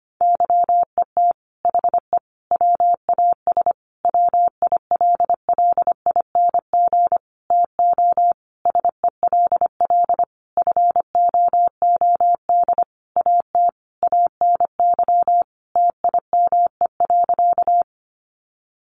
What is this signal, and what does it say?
Morse code, heard as a keyed tone, about 25 wpm, 700 Hz. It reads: YET HE WAS WILLING TO SELL FOOD AT ANY TIME.
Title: Sentences from Top 1100 Words 25wpm